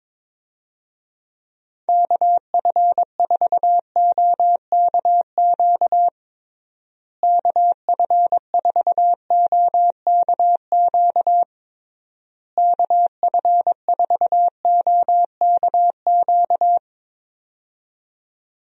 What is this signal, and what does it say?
Morse code, heard as a keyed tone, about 22 wpm, 700 Hz. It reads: KF4OKQ KF4OKQ KF4OKQ
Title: USA Call Signs - 6 Characters - Repeated 3x - 22wpm